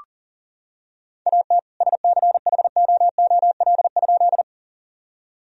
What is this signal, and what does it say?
Morse code, heard as a keyed tone, about 40 wpm, 700 Hz. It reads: AT SCHOOL?
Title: Sentences from Top 1000 Words - Review Part 1 - 40wpm